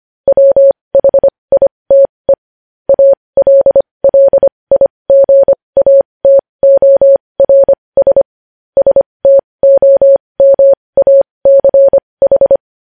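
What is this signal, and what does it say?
Morse code, heard as a keyed tone, about 25 wpm, spaced as for 20 wpm, 550 Hz. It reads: WHITE ALLIGATORS STOMACH
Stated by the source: Morse news headlines practice at 20 wpm.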